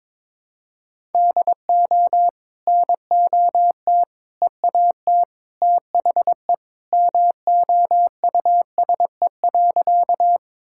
Morse code, heard as a keyed tone, about 22 wpm, 700 Hz. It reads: DO NOT EAT THE MOUSE.
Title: Sentences from Top 1000 Words - Review Part 5 - 22wpm